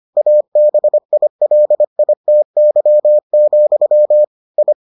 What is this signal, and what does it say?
Morse code, heard as a keyed tone, about 25 wpm, 600 Hz. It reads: ABILITY, I